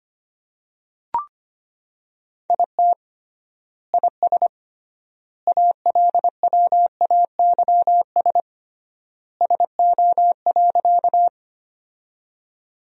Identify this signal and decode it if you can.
Morse code, heard as a keyed tone, about 25 wpm, 700 Hz. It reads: IT IS ALWAYS SO.
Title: 3x Word-Spacing - Sentences from Top 300 Words - 25wpm